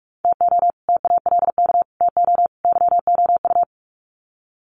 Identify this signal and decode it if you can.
Morse code, heard as a keyed tone, about 44 wpm, 700 Hz. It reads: TO TALK TO YOU
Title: Sets of 4 Words 45wpm